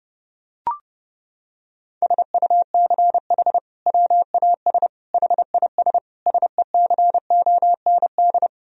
Morse code, heard as a keyed tone, about 32 wpm, 700 Hz. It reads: SUCH WAS HIS SECOND
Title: Sentences from Top 1000 Words - Review Part 1 - 30wpm